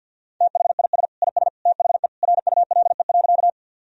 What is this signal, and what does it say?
Morse code, heard as a keyed tone, about 50 wpm, 700 Hz. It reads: THIS IS THE RULE.
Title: Sentences from Top 1000 Words - Review Part 2 - 50wpm